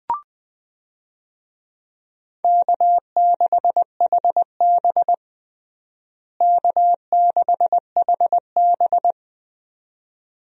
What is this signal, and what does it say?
Morse code, heard as a keyed tone, about 20 wpm, 700 Hz. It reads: K6HB K6HB
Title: USA Call Signs - 4 Characters - Repeated 2x - 20wpm